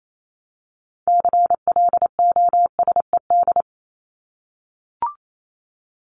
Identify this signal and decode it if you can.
Morse code, heard as a keyed tone, about 28 wpm, 700 Hz. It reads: CLOSED
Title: Top 1251-1500 Words 28wpm